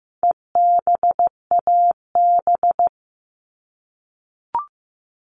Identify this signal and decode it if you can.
Morse code, heard as a keyed tone, about 15 wpm, 700 Hz. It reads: EBAB
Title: All Cut Numbers - 4-Digit 15wpm